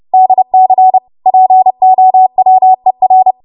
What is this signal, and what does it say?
Morse code, heard as a keyed tone, about 30 wpm, 750 Hz. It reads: DC POWER